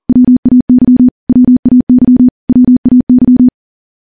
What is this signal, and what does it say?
Morse code, heard as a keyed tone, about 42 wpm, 250 Hz. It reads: WAY WAY WAY
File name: morse / morse-training-40WPM